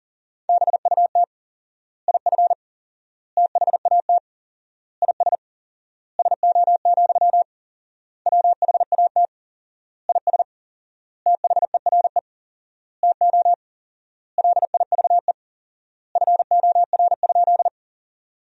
Morse code, heard as a keyed tone, about 40 wpm, 700 Hz. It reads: BUT IF THAT IS SO, WHAT IS THERE TO LIVE FOR?
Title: 4x Word-Spacing - Sentences from Top 200 Words - 40wpm